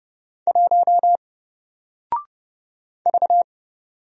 Morse code, heard as a keyed tone, about 30 wpm, 700 Hz. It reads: 1 V
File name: Single-Letter-Number-30wpm